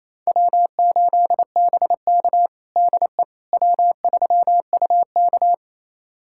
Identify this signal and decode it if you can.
Morse code, heard as a keyed tone, about 28 wpm, 700 Hz. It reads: W8BK DE W3UK
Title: USA Callsign DE Callsign - 4 Characters - 28wpm